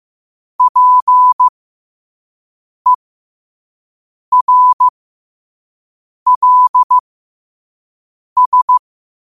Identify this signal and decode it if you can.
Morse code, heard as a keyed tone, about 15 wpm, spaced as for 5 wpm, 1000 Hz. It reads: PERLS
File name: qotdfiles / 5wpm